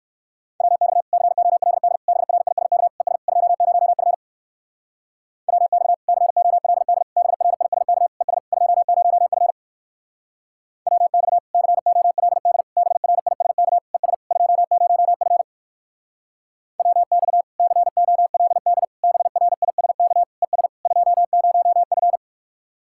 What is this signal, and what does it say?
Morse code, heard as a keyed tone, about 53 wpm, 700 Hz. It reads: WX COLD BRISK ES 20F WX COLD BRISK ES 20F WX COLD BRISK ES 20F WX COLD BRISK ES 20F
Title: Speed-Racing Common QSO Phrases Encore - 45wpm